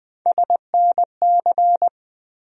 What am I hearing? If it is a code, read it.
Morse code, heard as a keyed tone, about 20 wpm, 700 Hz. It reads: SNC
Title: USA Call Sign Postfixes - 3 Letters 20wpm